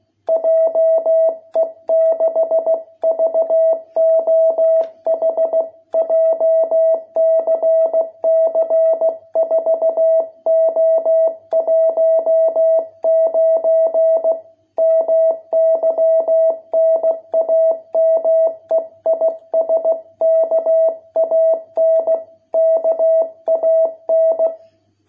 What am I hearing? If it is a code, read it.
Morse code, heard as a keyed tone, about 16 wpm, 660 Hz. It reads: JE6VOH JCC 4O19 MY NAME IS KAN KAN